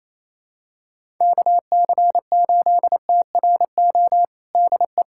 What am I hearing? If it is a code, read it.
Morse code, heard as a keyed tone, about 28 wpm, 700 Hz. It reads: KC8TRO DE
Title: USA Callsign DE Callsign - 6 Characters - 28wpm